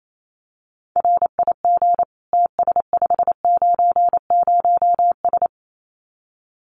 Morse code, heard as a keyed tone, about 28 wpm, 700 Hz. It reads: RIG TS590S
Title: Common QSO Phrases Encore 28wpm